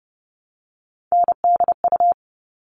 Morse code, heard as a keyed tone, about 30 wpm, 700 Hz. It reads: NDU
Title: All Cut Numbers - 3-Digit 30wpm